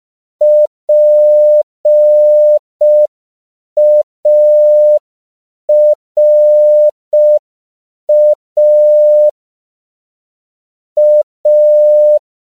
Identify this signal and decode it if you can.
Morse code, heard as a keyed tone, about 5 wpm, 600 Hz. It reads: PARA A
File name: CW / 5PPM